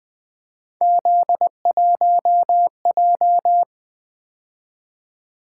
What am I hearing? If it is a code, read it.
Morse code, heard as a keyed tone, about 20 wpm, 700 Hz. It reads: Z1J
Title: Three Letter-Number Combinations - 20wpm